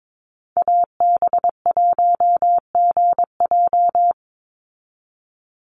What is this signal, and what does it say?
Morse code, heard as a keyed tone, about 22 wpm, 690 Hz. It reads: AB1GJ
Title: Call Signs Repeated 3x - 22wpm